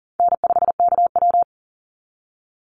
Morse code, heard as a keyed tone, about 40 wpm, 700 Hz. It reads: N5KW
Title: Call Signs - 40wpm